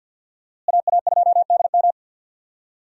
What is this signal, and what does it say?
Morse code, heard as a keyed tone, about 51 wpm, 700 Hz. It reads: AA2DM
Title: USA Call Signs - 5 Characters - Repeated 2x - 50wpm